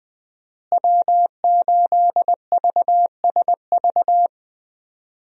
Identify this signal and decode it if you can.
Morse code, heard as a keyed tone, about 20 wpm, 700 Hz. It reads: W8VSV
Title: USA Call Signs - 5 Characters - Repeated 2x - 20wpm